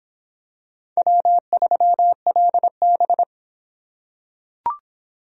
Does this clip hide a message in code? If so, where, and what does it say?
Yes: Morse code from 1.0 to 3.2 s, heard as a keyed tone, about 23 wpm, 700 Hz. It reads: W3LB